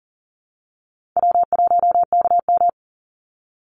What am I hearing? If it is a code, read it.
Morse code, heard as a keyed tone, about 40 wpm, 700 Hz. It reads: W1KM
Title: Easy Call Signs Repeated 2x - 40wpm